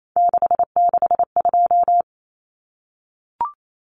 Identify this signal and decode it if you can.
Morse code, heard as a keyed tone, about 28 wpm, 700 Hz. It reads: 662
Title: Common Cut Numbers - 3-Digit 28wpm